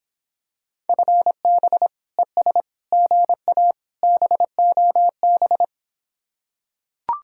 Morse code, heard as a keyed tone, about 26 wpm, 700 Hz. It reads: FB ES GA BOB